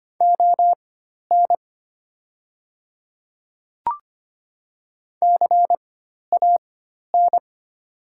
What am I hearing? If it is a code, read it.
Morse code, heard as a keyed tone, about 25 wpm, spaced as for 12 wpm, 700 Hz. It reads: ON CAN
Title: 4x Character Spacing - Top 100 Words - 25wpm